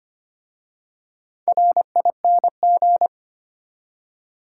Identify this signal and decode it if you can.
Morse code, heard as a keyed tone, about 25 wpm, 700 Hz. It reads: RING
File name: practice / 25wpm / Easy Words with Hints - 25wpm